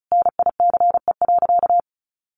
Morse code, heard as a keyed tone, about 34 wpm, 700 Hz. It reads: NICE.
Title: Sentences from Top 1100 Words 35wpm